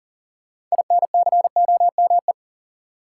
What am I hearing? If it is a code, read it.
Morse code, heard as a keyed tone, about 40 wpm, 700 Hz. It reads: INCOME
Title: Sentences from Top 1000 Words - Review Part 5 - New Word Drill - 40wpm